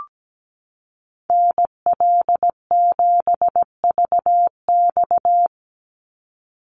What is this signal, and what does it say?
Morse code, heard as a keyed tone, about 17 wpm, 700 Hz. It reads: NL7VX
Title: Call Signs - 17wpm